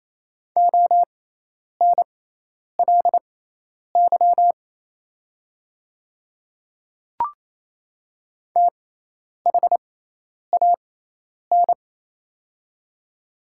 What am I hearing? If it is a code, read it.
Morse code, heard as a keyed tone, about 27 wpm, spaced as for 10 wpm, 700 Hz. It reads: ONLY THAN
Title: 6x Character Spacing - Top 200 Words - 28wpm